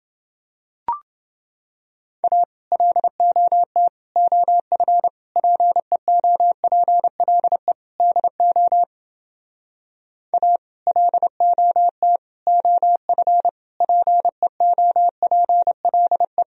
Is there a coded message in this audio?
Yes: Morse code from 2.2 to 16.4 s, heard as a keyed tone, about 30 wpm, 700 Hz. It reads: A LOT OF PEOPLE DO A LOT OF PEOPLE